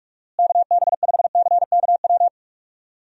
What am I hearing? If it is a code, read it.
Morse code, heard as a keyed tone, about 46 wpm, 700 Hz. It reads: KB5CKW
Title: USA Call Signs - 6 Characters - Repeated 2x - 45wpm